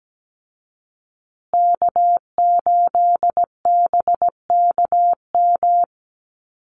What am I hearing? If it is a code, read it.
Morse code, heard as a keyed tone, about 17 wpm, 700 Hz. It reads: K8BKM
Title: Call Signs Repeated 2x - 17wpm